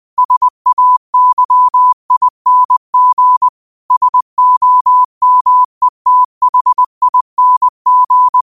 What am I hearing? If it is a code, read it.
Morse code, heard as a keyed tone, about 20 wpm, 1000 Hz. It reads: SAYING SOMETHING
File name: qotdfiles / 20wpm